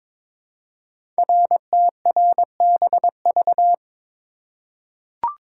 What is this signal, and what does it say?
Morse code, heard as a keyed tone, about 22 wpm, 700 Hz. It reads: RTRBV
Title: Code Groups - 5 Letters - 22wpm